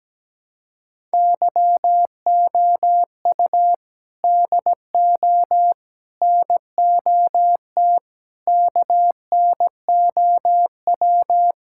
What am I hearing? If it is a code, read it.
Morse code, heard as a keyed tone, about 17 wpm, 700 Hz. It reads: YOU DO NOT KNOW
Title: Sentences from Top 1000 Words - Review Part 6 - 17wpm